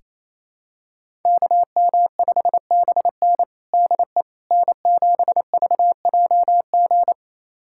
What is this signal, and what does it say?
Morse code, heard as a keyed tone, about 28 wpm, 700 Hz. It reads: KM5BN DE N7VJG
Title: USA Callsign DE Callsign - 5 Characters - 28wpm